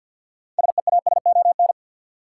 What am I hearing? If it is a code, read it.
Morse code, heard as a keyed tone, about 47 wpm, 700 Hz. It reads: SEASON
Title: Sentences from Top 1400 Words - New Word Drill - 50wpm